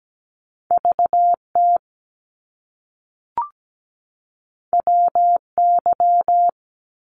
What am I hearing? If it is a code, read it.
Morse code, heard as a keyed tone, about 17 wpm, 700 Hz. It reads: VT WY
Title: US States 17wpm